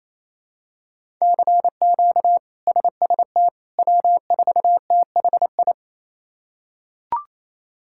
Morse code, heard as a keyed tone, about 28 wpm, 700 Hz. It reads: CQ SST W4THI
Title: K1USN SST QSO Practice - 28wpm